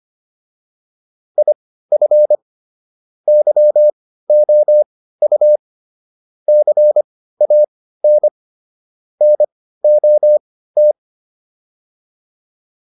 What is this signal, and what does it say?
Morse code, heard as a keyed tone, about 25 wpm, spaced as for 15 wpm, 600 Hz. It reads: IF YOU CAN NOT